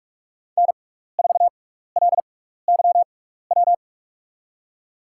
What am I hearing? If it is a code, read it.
Morse code, heard as a keyed tone, about 44 wpm, spaced as for 15 wpm, 700 Hz. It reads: N4LYW
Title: USA Call Signs - 5 Characters - 6x Character Spacing - 45wpm